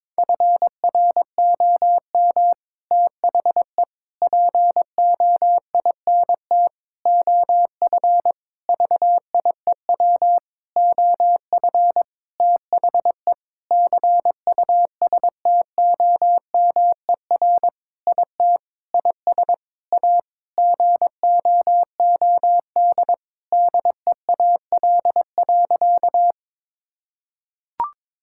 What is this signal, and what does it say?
Morse code, heard as a keyed tone, about 22 wpm, 700 Hz. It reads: FROM THE POINT OF VIEW OF THE CUSTOMER IT IS A GOOD DEAL.
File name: Sentences with 7-Word Phrases - 22wpm